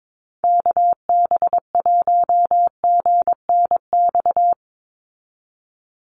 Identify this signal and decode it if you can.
Morse code, heard as a keyed tone, about 22 wpm, 700 Hz. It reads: KB1GNX
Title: USA Call Signs 22wpm